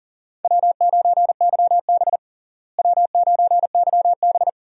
Morse code, heard as a keyed tone, about 40 wpm, 700 Hz. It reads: W9YB W9YB
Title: USA Call Signs - 4 Characters - Repeated 2x - 40wpm